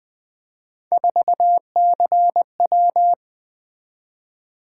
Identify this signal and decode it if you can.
Morse code, heard as a keyed tone, about 20 wpm, 700 Hz. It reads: 4CW
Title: Three Letter-Number Combinations - 20wpm